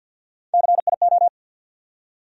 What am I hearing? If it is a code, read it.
Morse code, heard as a keyed tone, about 47 wpm, 700 Hz. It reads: CIO
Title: USA Call Sign Postfixes - 3 Letters 50wpm